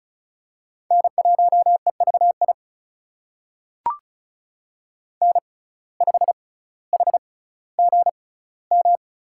Morse code, heard as a keyed tone, about 36 wpm, spaced as for 18 wpm, 700 Hz. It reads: N1EVI N5HGM
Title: USA Call Signs - 6x Character Spacing - 35wpm